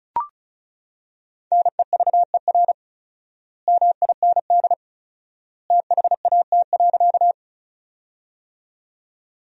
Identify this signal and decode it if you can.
Morse code, heard as a keyed tone, about 34 wpm, 700 Hz. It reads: NEVER MIND THAT.
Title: 4x Word-Spacing - Sentences from Top 500 Words - 35wpm